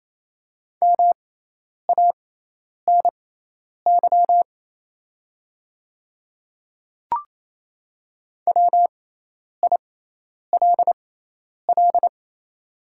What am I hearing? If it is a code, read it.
Morse code, heard as a keyed tone, about 28 wpm, spaced as for 10 wpm, 700 Hz. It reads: MANY WILL
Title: 6x Character Spacing - Top 100 Words - 28wpm